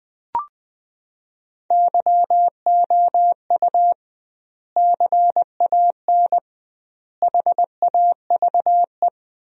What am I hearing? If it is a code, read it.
Morse code, heard as a keyed tone, about 21 wpm, 700 Hz. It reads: YOU CAN HAVE
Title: 2x Word-Spacing - Sentences from Top 100 Words - 20wpm